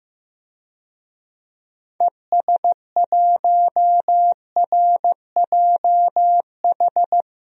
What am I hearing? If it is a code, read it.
Morse code, heard as a keyed tone, about 15 wpm, 700 Hz. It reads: ES1RJH
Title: DXCC-100 Call Signs - 6 Characters - Repeated 2x - 15wpm